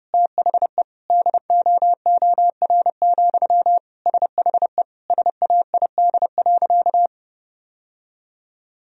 Morse code, heard as a keyed tone, about 30 wpm, 700 Hz. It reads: THE DOOR, SHE SAID.
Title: Sentences from Top 1000 Words - Review Part 2 - 30wpm